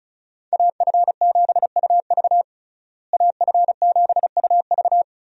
Morse code, heard as a keyed tone, about 34 wpm, 700 Hz. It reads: AF7UV AF7UV
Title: USA Call Signs - 5 Characters - Repeated 2x - 35wpm